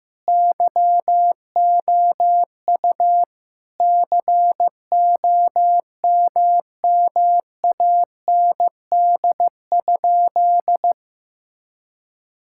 Morse code, heard as a keyed tone, about 15 wpm, 700 Hz. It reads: YOU COMMAND?
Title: Sentences from Top 1600 Words 15wpm